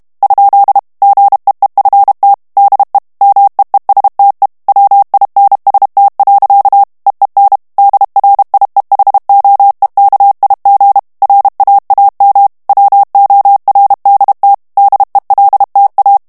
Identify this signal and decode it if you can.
Morse code, heard as a keyed tone, about 32 wpm, 800 Hz. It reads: ? GEEFT DE MEESTE WINST. EEN DRIEHOEKIG RAAM WORDT DELTA
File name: HOLLANDAIS texte clair 1 vitesse 30 mots minute